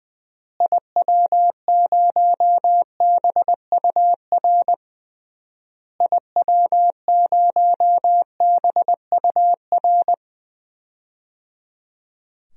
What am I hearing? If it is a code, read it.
Morse code, heard as a keyed tone, about 20 wpm, 700 Hz. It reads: IW0BUR IW0BUR
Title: DXCC-100 Call Signs - 6 Characters - Repeated 3x - 20wpm